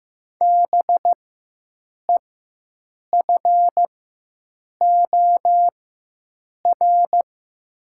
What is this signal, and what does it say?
Morse code, heard as a keyed tone, about 15 wpm, spaced as for 7 wpm, 700 Hz. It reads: BEFOR
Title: 4x Character Spacing - Top 500 Words - 15wpm